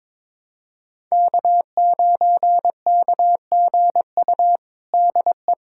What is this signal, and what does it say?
Morse code, heard as a keyed tone, about 22 wpm, 700 Hz. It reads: K9KGU DE
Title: USA Callsign DE Callsign - 5 Characters - 22wpm